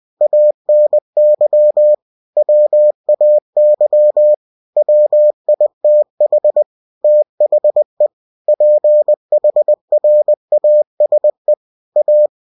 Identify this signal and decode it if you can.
Morse code, heard as a keyed tone, about 20 wpm, 600 Hz. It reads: ANY WAY WITH THE PHRASE A
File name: cw-books / winnie-the-pooh / winnie-20wpm-page-0113